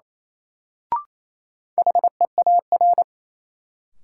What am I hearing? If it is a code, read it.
Morse code, heard as a keyed tone, about 28 wpm, 700 Hz. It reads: HEAR
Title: Sentences from Top 300 Words - New Word Drill - 28wpm